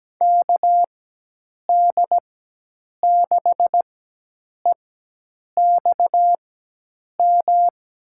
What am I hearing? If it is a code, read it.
Morse code, heard as a keyed tone, about 17 wpm, spaced as for 8 wpm, 700 Hz. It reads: KD6EXM
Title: USA Call Signs - 4x Character Spacing - 17wpm